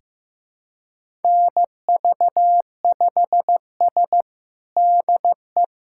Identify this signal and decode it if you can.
Morse code, heard as a keyed tone, about 15 wpm, 700 Hz. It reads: NV5S DE
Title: USA Callsign DE Callsign - 4 Characters - 15wpm